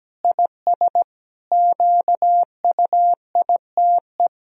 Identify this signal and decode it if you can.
Morse code, heard as a keyed tone, about 17 wpm, 700 Hz. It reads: IS QUITE
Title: Sentences from Top 1000 Words - Review Part 2 - 17wpm